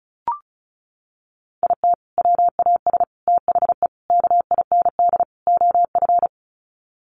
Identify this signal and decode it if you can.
Morse code, heard as a keyed tone, about 34 wpm, 700 Hz. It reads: IT WAS THE KIND OF
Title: Sets of 5 Words Encore2 35wpm